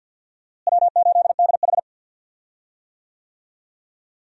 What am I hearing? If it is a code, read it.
Morse code, heard as a keyed tone, about 46 wpm, 700 Hz. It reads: W8DH
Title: USA Call Signs - 4 Characters - Repeated 2x - 50wpm